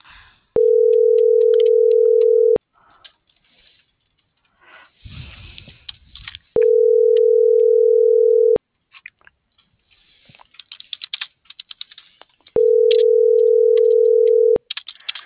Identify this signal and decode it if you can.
Ringback tone (North America), 440 + 480 Hz.